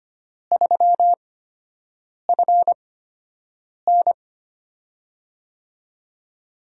Morse code, heard as a keyed tone, about 25 wpm, spaced as for 7 wpm, 700 Hz. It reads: 3FN
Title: USA Call Signs - 8x Character Spacing - 25wpm